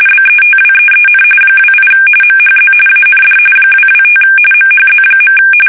RTTY, about 45.45 baud. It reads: BETWEEN 28 MHZ AND 225 MHZ.
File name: fsk850 / teletype-RTTY